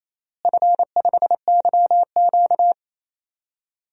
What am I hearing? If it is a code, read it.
Morse code, heard as a keyed tone, about 28 wpm, 700 Hz. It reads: F5YQ
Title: DXCC-100 Call Signs - 4 Characters - Repeated 3x - 28wpm